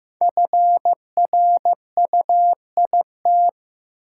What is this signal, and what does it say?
Morse code, heard as a keyed tone, about 15 wpm, 700 Hz. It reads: FRUIT
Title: Sentences from Top 1300 Words - New Word Drill Alt - 15wpm